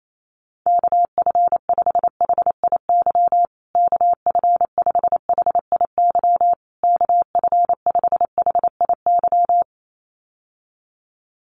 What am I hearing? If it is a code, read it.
Morse code, heard as a keyed tone, about 28 wpm, 700 Hz. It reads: KF5HIY KF5HIY KF5HIY
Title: USA Call Signs Repeated 3x 28wpm